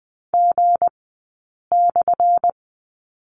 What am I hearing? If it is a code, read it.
Morse code, heard as a keyed tone, about 20 wpm, spaced as for 8 wpm, 700 Hz. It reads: G/
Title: 177 Lesson 29d - Slash - Two Characters 20wpm